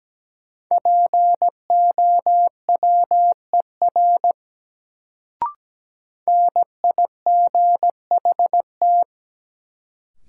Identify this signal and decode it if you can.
Morse code, heard as a keyed tone, about 17 wpm, 700 Hz. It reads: POWER NIGHT
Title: Sets of 4 Words - New Word Drill - 17wpm